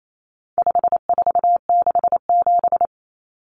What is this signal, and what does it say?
Morse code, heard as a keyed tone, about 28 wpm, 700 Hz. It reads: 5467
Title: Common Cut Numbers - 4-Digit 28wpm